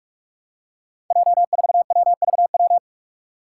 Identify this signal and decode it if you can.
Morse code, heard as a keyed tone, about 46 wpm, 700 Hz. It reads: J4WVW
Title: Five Challenging Letter-Number Combinations 45wpm